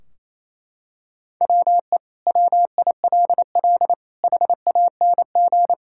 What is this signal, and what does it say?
Morse code, heard as a keyed tone, about 28 wpm, 700 Hz. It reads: WE WILL HANG
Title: Sentences from Top 1600 Words 28wpm